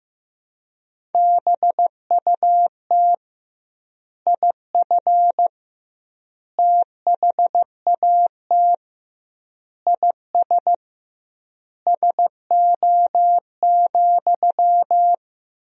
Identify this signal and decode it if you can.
Morse code, heard as a keyed tone, about 15 wpm, 700 Hz. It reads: BUT IF THAT IS SO,
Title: 2x Word-Spacing - Sentences from Top 200 Words - 15wpm